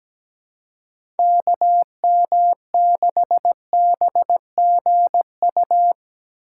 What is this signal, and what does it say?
Morse code, heard as a keyed tone, about 17 wpm, 700 Hz. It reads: KM6BGU
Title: USA Call Signs - 6 Characters - 3x Character Spacing - 17wpm